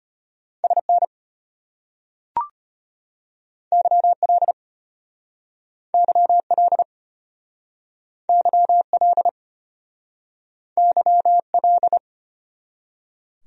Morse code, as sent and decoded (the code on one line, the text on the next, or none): ... -. / -.-- .-.. / -.-- .-.. / -.-- .-.. / -.-- .-..
SN YL YL YL YL